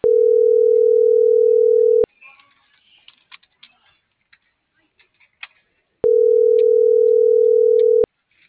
Ringback tone (North America), 440 + 480 Hz.